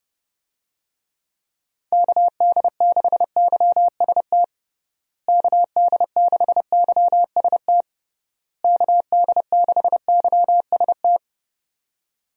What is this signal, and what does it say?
Morse code, heard as a keyed tone, about 30 wpm, 700 Hz. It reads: KD6YST KD6YST KD6YST
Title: USA Call Signs - 6 Characters - Repeated 3x - 30wpm